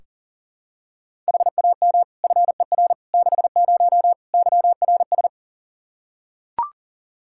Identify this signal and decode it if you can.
Morse code, heard as a keyed tone, about 37 wpm, 700 Hz. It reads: HAM FER 60 YRS
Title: Practice QSO - Exchange 3 - More Info - 40wpm